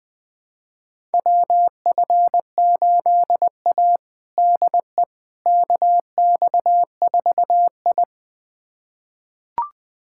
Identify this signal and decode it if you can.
Morse code, heard as a keyed tone, about 20 wpm, 700 Hz. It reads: WF8A DE KX4I
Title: USA Callsign DE Callsign - 4 Characters - 20wpm